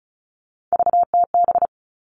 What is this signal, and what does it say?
Morse code, heard as a keyed tone, about 34 wpm, 700 Hz. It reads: VTB
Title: All Cut Numbers - 3-Digit 35wpm